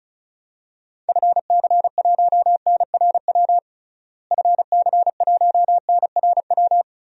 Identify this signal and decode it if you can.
Morse code, heard as a keyed tone, about 36 wpm, 700 Hz. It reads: FC1NRW FC1NRW
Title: DXCC-100 Call Signs - 6 Characters - Repeated 3x - 35wpm